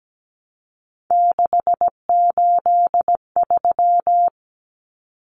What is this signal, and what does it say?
Morse code, heard as a keyed tone, about 17 wpm, 700 Hz. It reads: 683
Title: Numbers - 3-Digit 17wpm